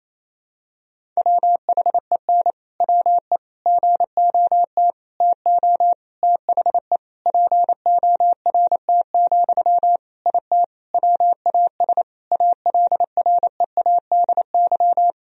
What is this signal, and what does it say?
Morse code, heard as a keyed tone, about 28 wpm, 700 Hz. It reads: WHEN WE GOT TO THE PORT, IT WAS ALREADY
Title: Sentences from Top 1000 Words - Review Part 1 - 28wpm